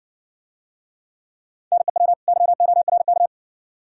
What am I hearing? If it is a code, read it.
Morse code, heard as a keyed tone, about 60 wpm, 700 Hz. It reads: NEW YORK